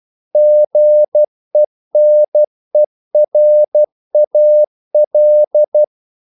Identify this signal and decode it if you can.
Morse code, heard as a keyed tone, about 12 wpm, 600 Hz. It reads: GENERAL
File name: CW / Practice / EnglishQuotations0112